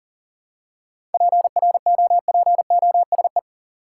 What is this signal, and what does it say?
Morse code, heard as a keyed tone, about 40 wpm, 700 Hz. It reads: PROPOSE
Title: Seven Letter Words - 40wpm